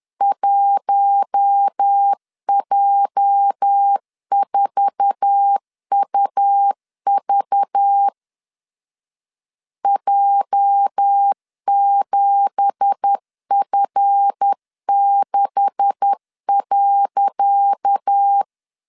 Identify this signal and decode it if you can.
Morse code, heard as a keyed tone, about 10 wpm, 800 Hz. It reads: 1J4UV J7F6.
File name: LEZIONI / 10 SIGNS / 10D COMA 10wpm 3013